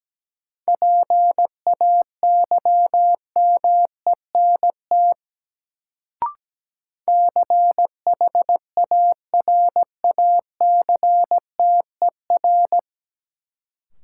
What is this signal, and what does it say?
Morse code, heard as a keyed tone, about 17 wpm, 700 Hz. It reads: PAYMENT CHARACTER
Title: Sentences from Top 1200 Words - New Word Drill Alt - 17wpm